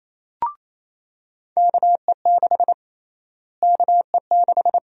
Morse code, heard as a keyed tone, about 28 wpm, 700 Hz. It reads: KE6 KE6
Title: USA Call Sign Prefixes - Repeated 2x - 28wpm